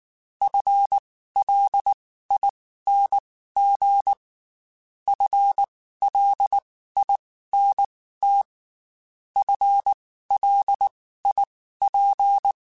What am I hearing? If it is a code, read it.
Morse code, heard as a keyed tone, about 19 wpm, spaced as for 14 wpm, 800 Hz. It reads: FLING FLINT FLIP
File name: Spaced / FL Runs sp mp3s / FL Runs sp @18